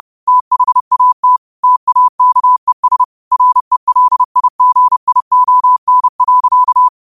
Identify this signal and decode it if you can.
Morse code, heard as a keyed tone, about 30 wpm, 1000 Hz. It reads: THAT TAKES RELIGION.
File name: qotdfiles / 30wpm